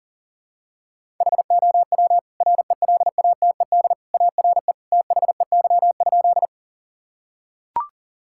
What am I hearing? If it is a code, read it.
Morse code, heard as a keyed tone, about 40 wpm, 700 Hz. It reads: HOW RELATED ARE THEY?
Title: Sentences from Top 1000 Words - Review Part 5 - 40wpm